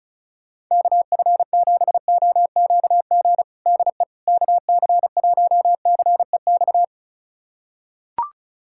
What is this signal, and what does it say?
Morse code, heard as a keyed tone, about 36 wpm, 700 Hz. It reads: KF7OQG DE KC1CEX
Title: USA Callsign DE Callsign - 6 Characters - 35wpm